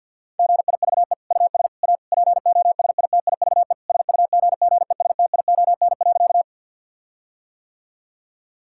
Morse code, heard as a keyed tone, about 50 wpm, 700 Hz. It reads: GIVE US A POSITIVE SUGGESTION.
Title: Sentences from Top 1600 Words 50wpm